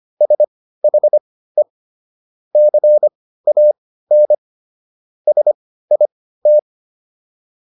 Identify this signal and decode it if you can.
Morse code, heard as a keyed tone, about 25 wpm, spaced as for 15 wpm, 600 Hz. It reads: SHE CAN SIT